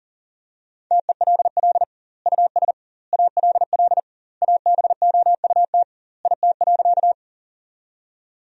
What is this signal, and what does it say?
Morse code, heard as a keyed tone, about 40 wpm, 700 Hz. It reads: TELL US ALL ABOUT IT.
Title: 2x Word-Spacing - Sentences from Top 200 Words - 40wpm